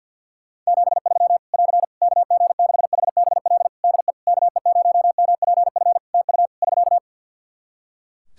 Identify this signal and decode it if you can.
Morse code, heard as a keyed tone, about 50 wpm, 700 Hz. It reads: 73 <AR> KG6HBL DE KE0MPF TU <SK>
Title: Practice QSO - Exchange 4 - Ending - 50wpm